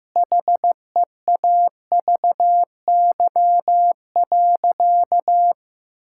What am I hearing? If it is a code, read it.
Morse code, heard as a keyed tone, about 15 wpm, 700 Hz. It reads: HEAVY.
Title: Sentences from Top 1600 Words 15wpm